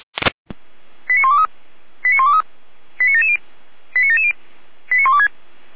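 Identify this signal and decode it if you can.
Five-tone selective call (ZVEI), heard as quick runs of five tones, tones of about 70 ms. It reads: 89123 89123 89770 89770 89126
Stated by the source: ZVEI five-tone selective call received by SDR.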